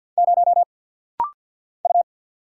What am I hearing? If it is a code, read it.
Morse code, heard as a keyed tone, about 50 wpm, 700 Hz. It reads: 0 U
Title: Single Letter-Number - Rapid-Fire 50wpm